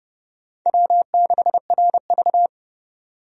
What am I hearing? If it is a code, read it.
Morse code, heard as a keyed tone, about 29 wpm, 700 Hz. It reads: W6RV
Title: USA Call Signs - 4 Characters - 30wpm